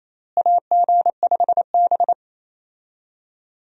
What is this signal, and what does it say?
Morse code, heard as a keyed tone, about 28 wpm, 700 Hz. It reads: AG5B
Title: USA Call Signs - 4 Characters - Repeated 2x - 28wpm